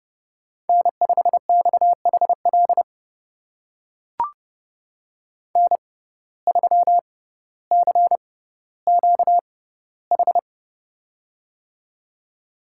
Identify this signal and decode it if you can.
Morse code, heard as a keyed tone, about 32 wpm, spaced as for 16 wpm, 700 Hz. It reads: N5XHL N3CQH
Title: USA Call Signs - 5 Characters - 6x Character Spacing - 30wpm